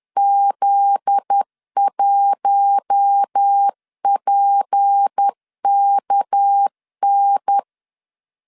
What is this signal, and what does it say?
Morse code, heard as a keyed tone, about 10 wpm, 800 Hz. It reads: Z1PKN
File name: LEZIONI / 9 2 10wpm 2736